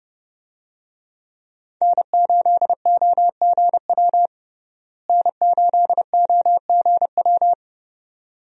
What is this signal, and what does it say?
Morse code, heard as a keyed tone, about 30 wpm, 700 Hz. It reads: N8OGW N8OGW
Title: USA Call Signs - 5 Characters - Repeated 2x - 30wpm